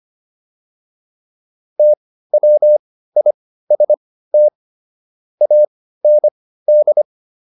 Morse code, heard as a keyed tone, about 25 wpm, spaced as for 15 wpm, 600 Hz. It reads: TWIST AND